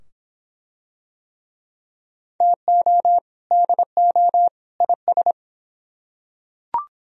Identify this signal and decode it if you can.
Morse code, heard as a keyed tone, about 22 wpm, 700 Hz. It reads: TO DO IS